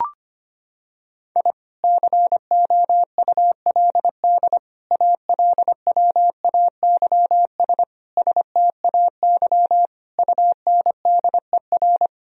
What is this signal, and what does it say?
Morse code, heard as a keyed tone, about 25 wpm, 700 Hz. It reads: I COULD ALWAYS STAY UNDER
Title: Sentences from Top 1000 Words - Review Part 1 - 25wpm